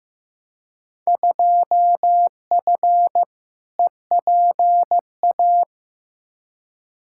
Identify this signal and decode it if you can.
Morse code, heard as a keyed tone, about 15 wpm, 700 Hz. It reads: 2F EPA
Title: Field Day Exchanges - 15wpm